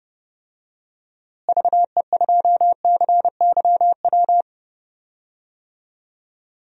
Morse code, heard as a keyed tone, about 30 wpm, 700 Hz. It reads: VE2CYW
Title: DXCC-100 Call Signs - 6 Characters - Repeated 3x - 30wpm